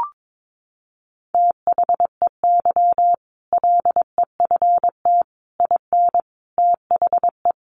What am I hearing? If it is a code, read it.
Morse code, heard as a keyed tone, about 22 wpm, 700 Hz. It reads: THEY LEFT IN THE
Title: Sentences from Top 400 Words 22wpm